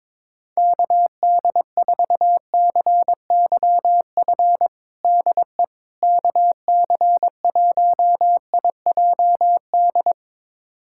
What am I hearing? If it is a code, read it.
Morse code, heard as a keyed tone, about 21 wpm, 700 Hz. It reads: KD4CYF DE KC1IJD
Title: USA Callsign DE Callsign - 6 Characters - 22wpm